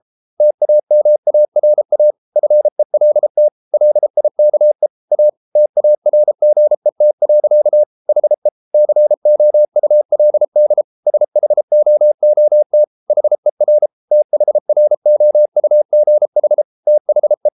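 Morse code, heard as a keyed tone, about 33 wpm, 600 Hz. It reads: TAMARA FELT LIKE A TARGET. HE COULD SHOOT HER THROUGH THE
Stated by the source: Morse code practice at 32 wpm from CW med Gnister, reading Ken Follett's novel Never in English.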